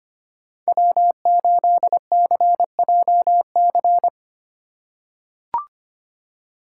Morse code, heard as a keyed tone, about 25 wpm, 700 Hz. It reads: W8CJC
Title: USA Call Signs - 5 Characters - 4x Character Spacing - 25wpm